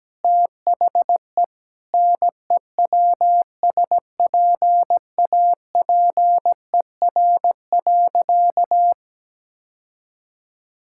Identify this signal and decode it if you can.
Morse code, heard as a keyed tone, about 17 wpm, 700 Hz. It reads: THE NEWSPAPER.
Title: Sentences from Top 1000 Words - Review Part 5 - 17wpm